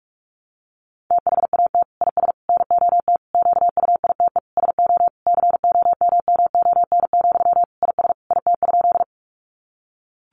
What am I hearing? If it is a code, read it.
Morse code, heard as a keyed tone, about 44 wpm, 700 Hz. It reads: THAT IS NOT QUITE SO COMMON, IS IT?
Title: Sentences from Top 700 Words 45wpm